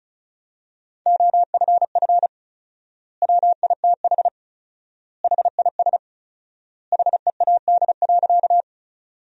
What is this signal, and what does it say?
Morse code, heard as a keyed tone, about 34 wpm, 700 Hz. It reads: OFF WITH HIS HEAD.
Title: 4x Word-Spacing - Sentences from Top 300 Words - 35wpm